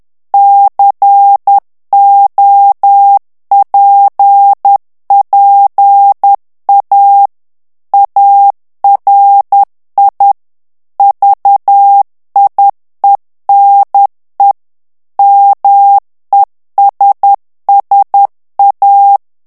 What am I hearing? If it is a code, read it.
Morse code, heard as a keyed tone, about 10 wpm, 800 Hz. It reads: COPPA ARI VIENE MESSA
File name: ITALIEN texte clair 2 vitesse 10 mots minute